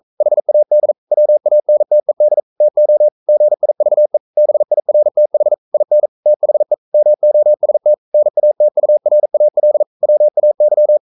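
Morse code, heard as a keyed tone, about 42 wpm, 600 Hz. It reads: HAD WANTED TO GIVE BIRTH IN THE MOST NATURAL WAY
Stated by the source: Morse code sending of an English novel at 42 wpm.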